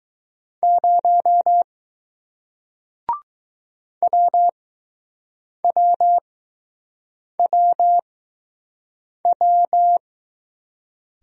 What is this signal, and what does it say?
Morse code, heard as a keyed tone, about 20 wpm, spaced as for 6 wpm, 700 Hz. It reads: 0 WWWW